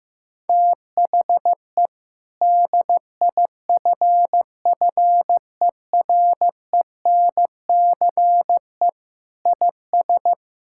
Morse code, heard as a keyed tone, about 16 wpm, 700 Hz. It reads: THE DIFFERENCE IS
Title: Sentences from Top 1000 Words - Review Part 5 - 15wpm